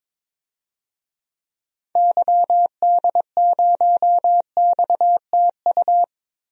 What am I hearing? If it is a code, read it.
Morse code, heard as a keyed tone, about 22 wpm, 700 Hz. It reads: YD0XTU
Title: DXCC-100 Call Signs - 6 Characters - 22wpm